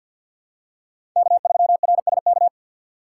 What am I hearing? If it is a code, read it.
Morse code, heard as a keyed tone, about 48 wpm, 700 Hz. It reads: K3RSK
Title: USA Call Signs - 5 Characters - 6x Character Spacing - 50wpm